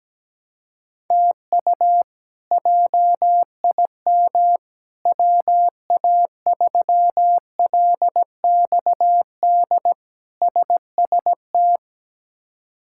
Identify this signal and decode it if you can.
Morse code, heard as a keyed tone, about 17 wpm, 700 Hz. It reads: TU JIM WA3LXD SST
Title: K1USN SST QSO Practice - 17wpm